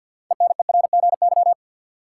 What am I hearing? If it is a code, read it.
Morse code, heard as a keyed tone, about 52 wpm, 700 Hz. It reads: ENERGY
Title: Six Letter Words - 50wpm